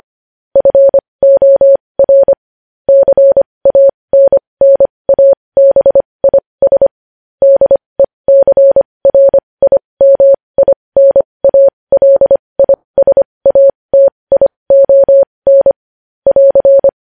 Morse code, heard as a keyed tone, about 25 wpm, spaced as for 20 wpm, 550 Hz. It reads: FOR CANNABIS DECRIMINALISATION <AR>
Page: news-headlines-in-morse-code-at-20-wpm